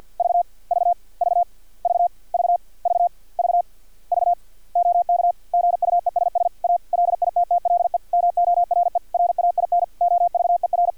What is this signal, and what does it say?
Morse code, heard as a keyed tone, about 50 wpm, 700 Hz. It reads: V V V V V V V V OK GUESS A LITTLE MORE RAIN OVER
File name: archived / CRSnet